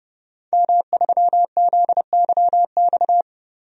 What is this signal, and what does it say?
Morse code, heard as a keyed tone, about 30 wpm, 700 Hz. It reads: M3ZYX
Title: DXCC-100 Call Signs - 5 Characters - 30wpm